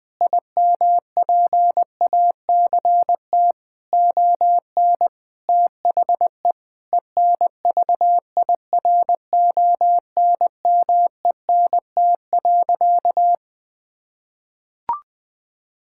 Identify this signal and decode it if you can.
Morse code, heard as a keyed tone, about 20 wpm, 700 Hz. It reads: IMPACT ON THE ENVIRONMENT.
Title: Sentences from Top 1000 Words - Review Part 6 - 20wpm